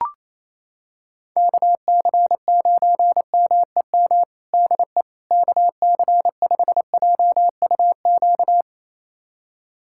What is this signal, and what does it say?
Morse code, heard as a keyed tone, about 28 wpm, 700 Hz. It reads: KC9MEM DE KC5JUQ
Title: USA Callsign DE Callsign - 6 Characters - 28wpm